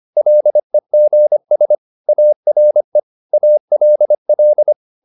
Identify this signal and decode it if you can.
Morse code, heard as a keyed tone, about 25 wpm, 600 Hz. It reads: LEGS ARE ALL